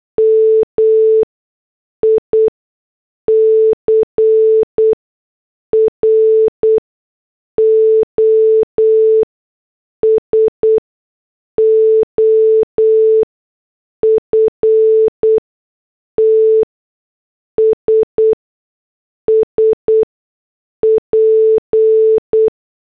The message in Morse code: MICROSOFTS SP